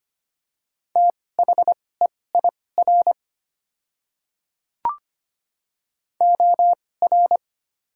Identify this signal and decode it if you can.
Morse code, heard as a keyed tone, about 25 wpm, spaced as for 18 wpm, 690 Hz. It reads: THEIR OR
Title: 2x Character Spacing - Top 100 Words - 25wpm